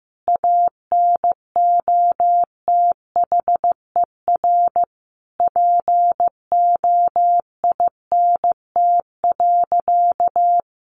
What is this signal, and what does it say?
Morse code, heard as a keyed tone, about 15 wpm, 700 Hz. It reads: ANOTHER POINT.
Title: Sentences from Top 600 Words 15wpm